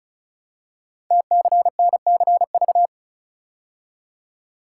Morse code, heard as a keyed tone, about 36 wpm, 690 Hz. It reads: TCNCV
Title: Code Groups - 5 Letters - 35wpm